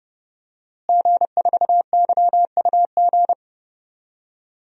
Morse code, heard as a keyed tone, about 30 wpm, 700 Hz. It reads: G4YUG
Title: DXCC-100 Call Signs - 5 Characters - Repeated 3x - 30wpm